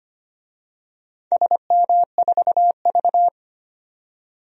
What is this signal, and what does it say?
Morse code, heard as a keyed tone, about 25 wpm, 700 Hz. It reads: SM4V
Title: DXCC-100 Call Signs - 4 Characters - 25wpm